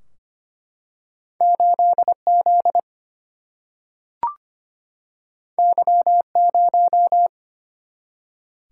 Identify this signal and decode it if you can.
Morse code, heard as a keyed tone, about 25 wpm, 700 Hz. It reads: 8Z Y0
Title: Single-Letter Single-Number 25wpm